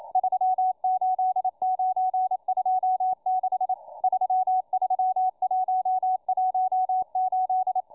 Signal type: Morse code